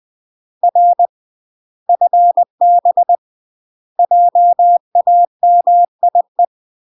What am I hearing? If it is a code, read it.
Morse code, heard as a keyed tone, about 20 wpm, 700 Hz. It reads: R FB JAMIE
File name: UK MASTER QSOs TRAINING / 20WPM / UK MASTER QSOs 0055